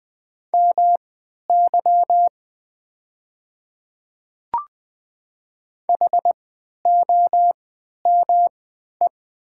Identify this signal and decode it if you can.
Morse code, heard as a keyed tone, about 20 wpm, spaced as for 11 wpm, 700 Hz. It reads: MY HOME
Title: 3x Character Spacing - Top 300 Words - 20wpm